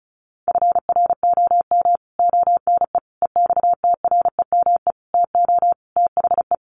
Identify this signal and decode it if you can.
Morse code, heard as a keyed tone, about 34 wpm, 700 Hz. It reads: FROM ONE EXTREME TO THE
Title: Sentences from Top 1300 Words 35wpm